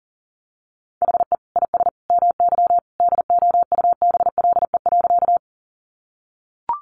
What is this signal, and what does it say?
Morse code, heard as a keyed tone, about 40 wpm, 700 Hz. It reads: HE IS MY DOUBLE.
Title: Sentences from Top 1400 Words 40wpm